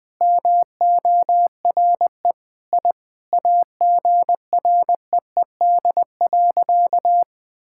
Morse code, heard as a keyed tone, about 20 wpm, 700 Hz. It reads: MORE I AGREED.
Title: Sentences with 9-Word Phrases - Spoken Beginning - 20wpm